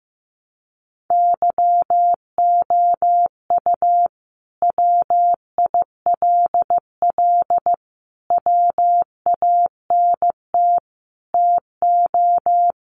Morse code, heard as a keyed tone, about 15 wpm, 700 Hz. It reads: YOU WILL WANT TO
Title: Sentences from Top 700 Words 15wpm